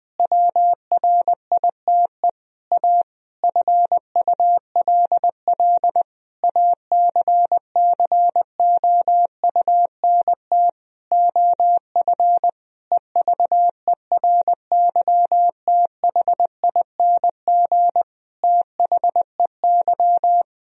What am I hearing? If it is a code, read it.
Morse code, heard as a keyed tone, about 21 wpm, 700 Hz. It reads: WRITE A FULL ACCOUNT OF EVERYTHING THEY
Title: Sentences from Top 1000 Words - Review Part 3 - 20wpm